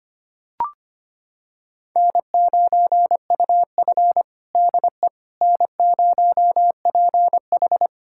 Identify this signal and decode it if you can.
Morse code, heard as a keyed tone, about 25 wpm, 700 Hz. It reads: N9UF DE N0PH
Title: USA Callsign DE Callsign - 4 Characters - 25wpm